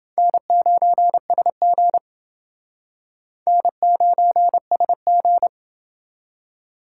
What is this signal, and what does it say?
Morse code, heard as a keyed tone, about 28 wpm, 700 Hz. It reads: N9SG N9SG